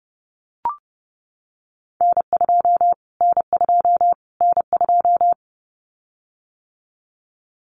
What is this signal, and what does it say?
Morse code, heard as a keyed tone, about 30 wpm, 700 Hz. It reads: N2 N2 N2
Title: Callsign Prefixes for Top 25 DXCC - Repeated 3x - 30wpm